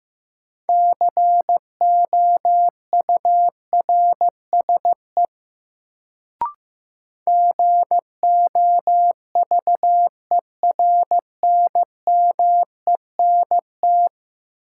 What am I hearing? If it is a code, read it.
Morse code, heard as a keyed tone, about 15 wpm, 700 Hz. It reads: COURSE GOVERNMENT
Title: Sets of 2 Words - New Word Drill - 15wpm